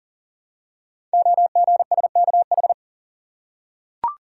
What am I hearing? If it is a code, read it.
Morse code, heard as a keyed tone, about 38 wpm, 700 Hz. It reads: OGSKH
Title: Code Groups - 5 Letters - 40wpm